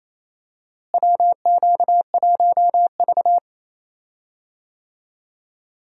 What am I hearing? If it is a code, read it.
Morse code, heard as a keyed tone, about 27 wpm, 700 Hz. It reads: WQ1V
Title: USA Call Signs - 4 Characters - Repeated 3x - 28wpm